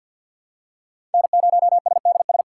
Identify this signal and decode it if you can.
Morse code, heard as a keyed tone, about 52 wpm, 700 Hz. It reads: N0SDS
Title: USA Call Signs - 5 Characters - 4x Character Spacing - 50wpm